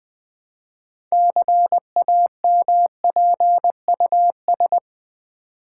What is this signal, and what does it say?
Morse code, heard as a keyed tone, about 20 wpm, 700 Hz. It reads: CAMPUS
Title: 6 Letter Words - First 3 Letters Given - 20wpm